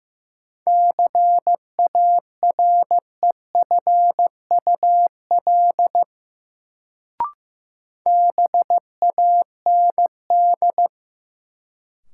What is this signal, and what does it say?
Morse code, heard as a keyed tone, about 15 wpm, 700 Hz. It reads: CAREFUL BAND
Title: Sentences from Top 1600 Words - New Word Drill - 15wpm